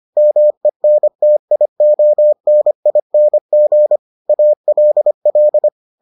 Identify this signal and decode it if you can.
Morse code, heard as a keyed tone, about 25 wpm, 600 Hz. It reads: MENTIONING ALL